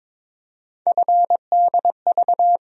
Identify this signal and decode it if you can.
Morse code, heard as a keyed tone, about 22 wpm, 700 Hz. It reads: FDV
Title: USA Call Sign Postfixes - 3 Letters 22wpm